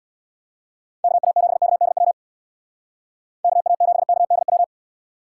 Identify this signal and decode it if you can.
Morse code, heard as a keyed tone, about 59 wpm, 700 Hz. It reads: KA7KDW KA7KDW
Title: Speed Racing - USA 2x3 Call Signs 50wpm